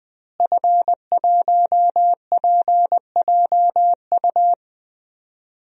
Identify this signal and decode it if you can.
Morse code, heard as a keyed tone, about 20 wpm, 700 Hz. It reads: F1PJU
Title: DXCC-100 Call Signs - 5 Characters - Repeated 2x - 20wpm